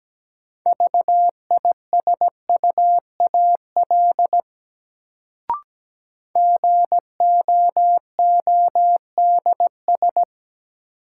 Morse code, heard as a keyed tone, about 17 wpm, 700 Hz. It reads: VISUAL GOODS
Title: Sentences from Top 1000 Words - Review Part 6 - New Word Drill - 17wpm